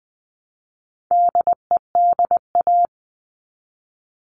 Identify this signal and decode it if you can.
Morse code, heard as a keyed tone, about 20 wpm, 700 Hz. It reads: DEDA
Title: All Cut Numbers - 4-Digit 20wpm